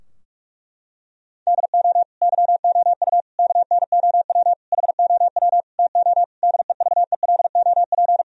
Morse code, heard as a keyed tone, about 41 wpm, 700 Hz. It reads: DO YOU KNOW HOW TO DEVELOP
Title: Sentences from Top 1000 Words - Review Part 3 - 45wpm